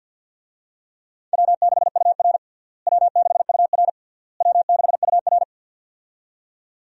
Morse code, heard as a keyed tone, about 51 wpm, 700 Hz. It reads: W6UR W6UR W6UR
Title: USA Call Signs - 4 Characters - Repeated 3x - 50wpm